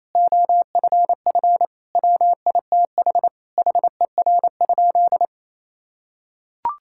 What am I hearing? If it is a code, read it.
Morse code, heard as a keyed tone, about 28 wpm, 700 Hz. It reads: OFF WITH HER?
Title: Sentences from Top 1000 Words - Review Part 6 - 28wpm